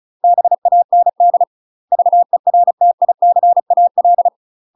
Morse code, heard as a keyed tone, about 34 wpm, 700 Hz. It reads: BAND VERTICAL
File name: UK MASTER QSOs TRAINING / 35WPM / UK MASTER QSOs 0004